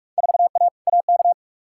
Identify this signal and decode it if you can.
Morse code, heard as a keyed tone, about 43 wpm, 700 Hz. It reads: 4A AK
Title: Field Day Exchanges - 45wpm